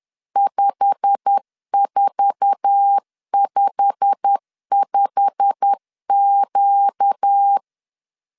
Morse code, heard as a keyed tone, about 10 wpm, 800 Hz. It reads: 5455Q
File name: LEZIONI / 8 YJ45 / 8 YJ45 10wpm 2549